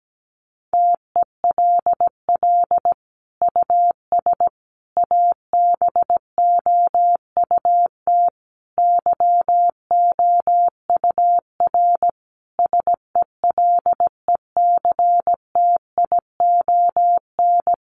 Morse code, heard as a keyed tone, about 17 wpm, 700 Hz. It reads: TELL US ABOUT YOUR SELECTION
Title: Sentences from Top 1400 Words 17wpm